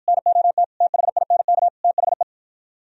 Morse code, heard as a keyed tone, about 50 wpm, 700 Hz. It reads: NOT THINK THE